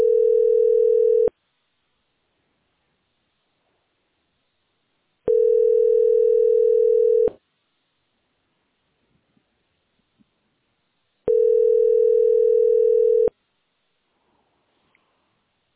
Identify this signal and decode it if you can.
Ringback tone (North America), 440 + 480 Hz.